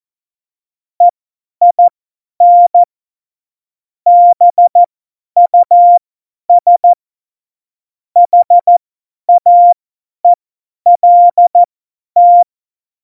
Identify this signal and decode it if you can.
Morse code, heard as a keyed tone, about 14 wpm, spaced as for 10 wpm, 700 Hz. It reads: EIN BUS HAELT
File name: Morsekurs / Witze / Witz 27. November 2018, 50 F70